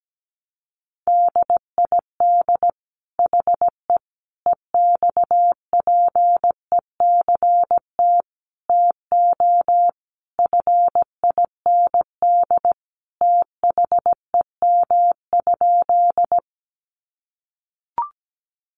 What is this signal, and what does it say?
Morse code, heard as a keyed tone, about 17 wpm, 700 Hz. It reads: DID HE EXPECT TO FIND THEM?
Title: Sentences from Top 1100 Words 17wpm